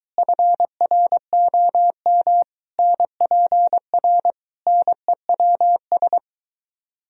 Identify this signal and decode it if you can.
Morse code, heard as a keyed tone, about 23 wpm, 700 Hz. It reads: FROM NPR NEWS